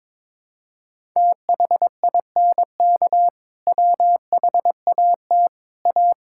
Morse code, heard as a keyed tone, about 22 wpm, 700 Hz. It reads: THINK WHAT A
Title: Sentences from Top 1000 Words - Review Part 5 - 22wpm